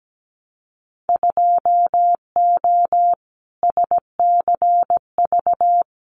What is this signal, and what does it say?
Morse code, heard as a keyed tone, about 17 wpm, 700 Hz. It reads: 2O SCV
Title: Winter Field Day Exchanges Encore 17wpm